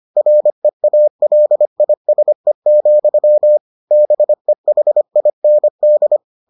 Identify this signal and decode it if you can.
Morse code, heard as a keyed tone, about 25 wpm, 600 Hz. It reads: REALISE, BEHIND